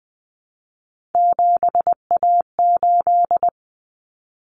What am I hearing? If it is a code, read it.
Morse code, heard as a keyed tone, about 20 wpm, 700 Hz. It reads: 7A8
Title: Common Cut Numbers - 3-Digit 20wpm